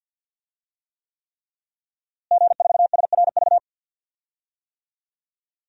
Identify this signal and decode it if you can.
Morse code, heard as a keyed tone, about 51 wpm, 700 Hz. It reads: G4SRV
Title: DXCC-100 Call Signs - 5 Characters - Repeated 3x - 50wpm